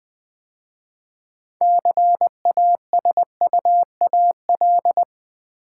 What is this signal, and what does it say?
Morse code, heard as a keyed tone, about 20 wpm, 700 Hz. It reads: CASUAL
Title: 6 Letter Words - First 3 Letters Given - 20wpm